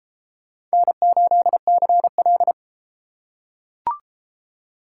Morse code, heard as a keyed tone, about 31 wpm, 700 Hz. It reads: N8CL